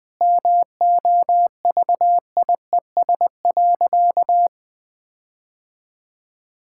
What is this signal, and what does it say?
Morse code, heard as a keyed tone, about 20 wpm, 700 Hz. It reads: MOVIES.
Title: Sentences with 4-Word Phrases - Spoken Beginning - 20wpm